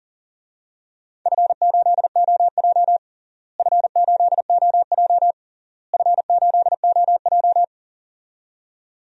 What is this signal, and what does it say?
Morse code, heard as a keyed tone, about 37 wpm, 700 Hz. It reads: F8OJ F8OJ F8OJ
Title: DXCC-100 Call Signs - 4 Characters - Repeated 3x - 40wpm